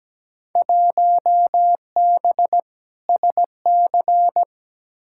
Morse code, heard as a keyed tone, about 17 wpm, 700 Hz. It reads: 1B SC
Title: Field Day Exchanges Encore - 17wpm